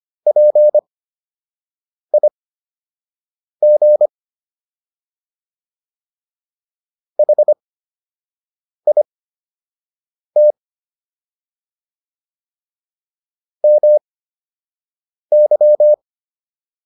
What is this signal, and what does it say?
Morse code, heard as a keyed tone, about 25 wpm, spaced as for 6 wpm, 600 Hz. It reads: PIG HIT MY